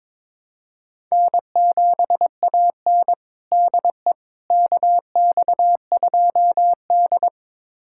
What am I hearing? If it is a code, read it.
Morse code, heard as a keyed tone, about 22 wpm, 700 Hz. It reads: N7AN DE KX2D
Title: USA Callsign DE Callsign - 4 Characters - 22wpm